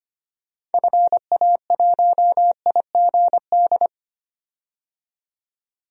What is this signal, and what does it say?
Morse code, heard as a keyed tone, about 25 wpm, 700 Hz. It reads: FA1IGD
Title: DXCC-100 Call Signs - 6 Characters - 25wpm